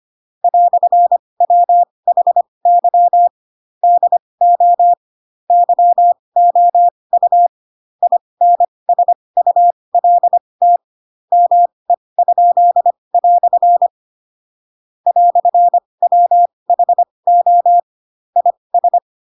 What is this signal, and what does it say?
Morse code, heard as a keyed tone, about 25 wpm, spaced as for 20 wpm, 700 Hz. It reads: "WHY DO YOU INSULT ME?" "WHO IS
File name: Pinocchio / Pinocchio ch2 15 25 ik1hge